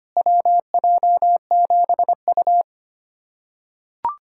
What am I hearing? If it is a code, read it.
Morse code, heard as a keyed tone, about 25 wpm, 700 Hz. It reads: WJ7U
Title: USA Callsign DE Callsign - 4 Characters - 25wpm